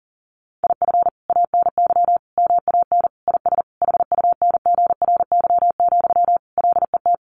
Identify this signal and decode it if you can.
Morse code, heard as a keyed tone, about 40 wpm, 700 Hz. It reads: IF ANY MAN IS HUNGRY, LET
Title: Sentences from Top 1400 Words 40wpm